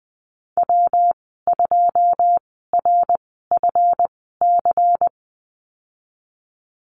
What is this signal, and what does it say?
Morse code, heard as a keyed tone, about 20 wpm, spaced as for 15 wpm, 700 Hz. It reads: W2RFC
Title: 167 Lesson 27g - Number 7 - Call Signs 20wpm